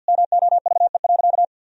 Morse code, heard as a keyed tone, about 50 wpm, 700 Hz. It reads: MOVE.
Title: Sentences from Top 1000 Words - Review Part 2 - 50wpm